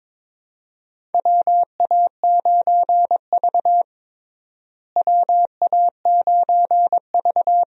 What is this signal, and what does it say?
Morse code, heard as a keyed tone, about 22 wpm, 700 Hz. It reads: WA9V WA9V
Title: USA Call Signs - 4 Characters - Repeated 2x - 22wpm